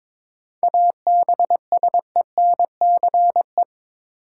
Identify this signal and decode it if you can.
Morse code, heard as a keyed tone, about 22 wpm, 700 Hz. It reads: ABSENCE
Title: Sets of 4 Words - New Word Drill - 22wpm